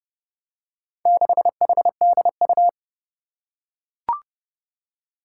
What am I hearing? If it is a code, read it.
Morse code, heard as a keyed tone, about 30 wpm, 700 Hz. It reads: 6HDU
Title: Four Challenging Letter-Number Combinations 30wpm